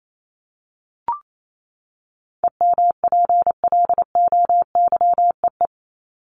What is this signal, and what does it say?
Morse code, heard as a keyed tone, about 28 wpm, 700 Hz. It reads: EMPLOYEE
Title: Top 1501-1750 Words 28wpm